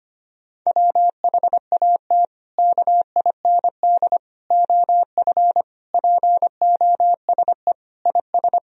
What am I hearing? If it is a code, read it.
Morse code, heard as a keyed tone, about 25 wpm, 700 Hz. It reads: WHAT KIND OF POSE IS
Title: Sentences from Top 1000 Words - Review Part 2 - 25wpm